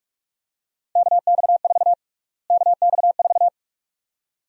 Morse code, heard as a keyed tone, about 45 wpm, 700 Hz. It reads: KX4 KX4
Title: USA Call Sign Prefixes - Repeated 2x - 45wpm